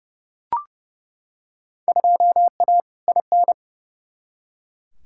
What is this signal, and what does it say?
Morse code, heard as a keyed tone, about 30 wpm, 700 Hz. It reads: 2A IN
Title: Field Day Exchanges - 30wpm